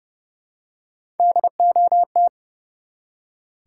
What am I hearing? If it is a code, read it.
Morse code, heard as a keyed tone, about 31 wpm, 700 Hz. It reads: DOT
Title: Three Letter Words - 30wpm